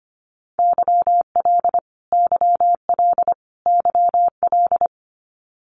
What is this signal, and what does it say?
Morse code, heard as a keyed tone, about 25 wpm, 700 Hz. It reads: YL YL YL
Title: Abbreviations Repeated 3x - 25wpm